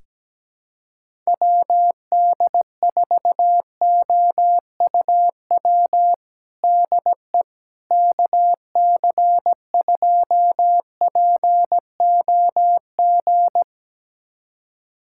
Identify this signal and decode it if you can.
Morse code, heard as a keyed tone, about 17 wpm, 700 Hz. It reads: WD4OUW DE KC2POG
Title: USA Callsign DE Callsign 17wpm